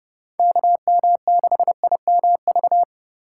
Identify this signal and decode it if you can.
Morse code, heard as a keyed tone, about 30 wpm, 700 Hz. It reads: KM6IMV
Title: USA Call Signs - 6 Characters - 6x Character Spacing - 30wpm